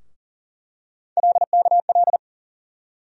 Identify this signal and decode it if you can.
Morse code, heard as a keyed tone, about 40 wpm, 700 Hz. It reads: LKL
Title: USA Call Sign Postfixes - 3 Letters 40wpm